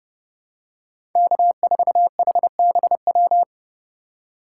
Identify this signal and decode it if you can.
Morse code, heard as a keyed tone, about 30 wpm, 700 Hz. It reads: K4HBW
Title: USA Call Signs - 5 Characters - 6x Character Spacing - 30wpm